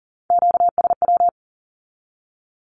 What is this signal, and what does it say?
Morse code, heard as a keyed tone, about 38 wpm, 700 Hz. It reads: QSW
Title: Q-Codes with Spoken Meaning - 40wpm